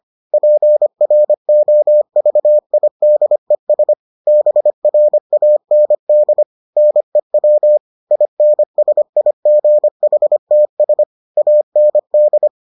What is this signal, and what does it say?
Morse code, heard as a keyed tone, about 25 wpm, 600 Hz. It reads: PROVIDES BRAND NEW INSIGHTS AND